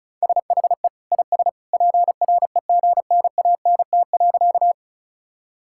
Morse code, heard as a keyed tone, about 34 wpm, 690 Hz. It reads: SHE IS PREGNANT.
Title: Sentences from Top 1000 Words - Review Part 6 - 35wpm